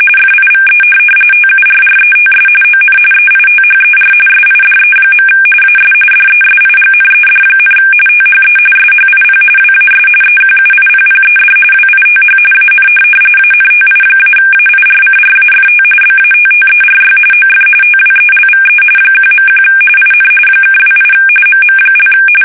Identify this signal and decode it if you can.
RTTY, about 45.45 baud. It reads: FIVE TURRETS ON EITHER SIDE. AS DESIGNED, THE SHIP WAS EQUIPPED WITH AN ANTI-AIRCRAFT BATTERY OF TWELVE 1.1 IN (28 MM)
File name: fsk850 / USS-Massachusetts